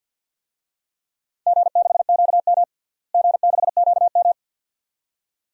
Morse code, heard as a keyed tone, about 52 wpm, 700 Hz. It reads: G6QM G6QM
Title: DXCC-100 Call Signs - 4 Characters - Repeated 2x - 50wpm